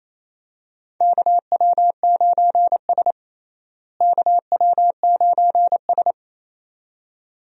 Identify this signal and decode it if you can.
Morse code, heard as a keyed tone, about 29 wpm, 700 Hz. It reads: KW9S KW9S
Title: USA Call Signs - 4 Characters - Repeated 2x - 28wpm